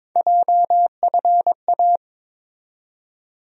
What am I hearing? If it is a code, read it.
Morse code, heard as a keyed tone, about 21 wpm, 700 Hz. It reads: JFA
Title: USA Call Sign Postfixes - 3 Letters 22wpm